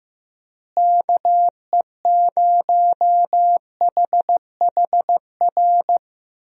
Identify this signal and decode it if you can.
Morse code, heard as a keyed tone, about 16 wpm, 700 Hz. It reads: KE0HHR
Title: USA Call Signs - 6 Characters - Repeated 3x - 15wpm